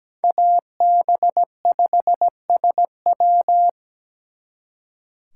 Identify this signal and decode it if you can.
Morse code, heard as a keyed tone, about 20 wpm, 700 Hz. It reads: AB5SW
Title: Speed Racing - USA 2x2 Call Signs 17wpm